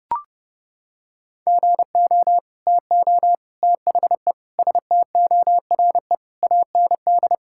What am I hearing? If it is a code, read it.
Morse code, heard as a keyed tone, about 30 wpm, 700 Hz. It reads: GO TO THE STORE AND
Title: Sentences from Top 1000 Words - Review Part 5 - 30wpm